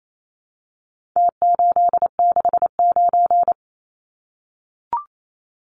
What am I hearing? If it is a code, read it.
Morse code, heard as a keyed tone, about 28 wpm, 700 Hz. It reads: T869
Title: Numbers - Contest Style - 4-Digit 28wpm